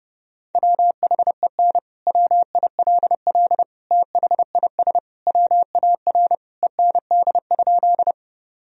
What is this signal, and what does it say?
Morse code, heard as a keyed tone, about 30 wpm, 700 Hz. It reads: WHEN WILL THIS WAR END?
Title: Sentences from Top 1000 Words - Review Part 2 - 30wpm